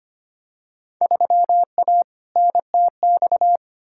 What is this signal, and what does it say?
Morse code, heard as a keyed tone, about 25 wpm, 700 Hz. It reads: 3A NTX
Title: Field Day Exchanges - 25wpm